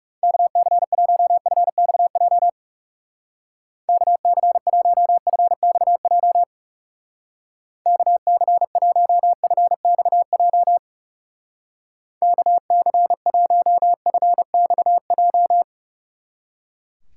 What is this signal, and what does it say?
Morse code, heard as a keyed tone, about 37 wpm, 700 Hz. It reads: KC1FXJ KC1FXJ KC1FXJ KC1FXJ
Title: Speed Racing - USA 2x3 Call Signs 30wpm